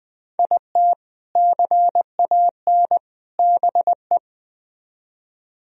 Morse code, heard as a keyed tone, about 20 wpm, 700 Hz. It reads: IT CAN BE
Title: Speed-Racing - Sets of 3 Words 17wpm